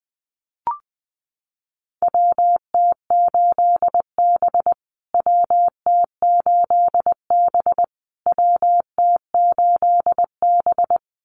Morse code, heard as a keyed tone, about 20 wpm, 700 Hz. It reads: WT8B WT8B WT8B
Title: USA Call Signs Repeated 3x 20wpm